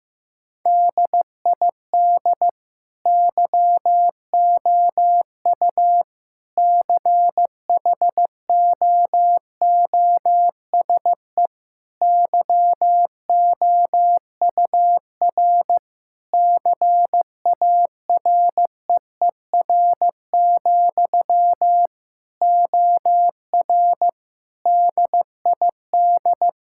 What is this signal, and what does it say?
Morse code, heard as a keyed tone, about 15 wpm, 700 Hz. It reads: DID YOU CHOOSE YOUR CAREER, OR DID
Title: Sentences from Top 1000 Words - Review Part 6 - 15wpm